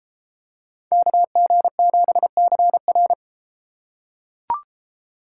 Morse code, heard as a keyed tone, about 31 wpm, 700 Hz. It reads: KG7CR